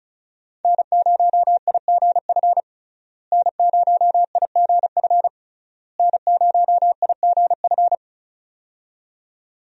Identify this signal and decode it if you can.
Morse code, heard as a keyed tone, about 36 wpm, 700 Hz. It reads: N0IGF N0IGF N0IGF
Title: USA Call Signs - 5 Characters - Repeated 3x - 35wpm